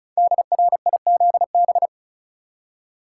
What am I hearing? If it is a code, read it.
Morse code, heard as a keyed tone, about 36 wpm, 690 Hz. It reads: DRIZB
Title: Code Groups - 5 Letters - 35wpm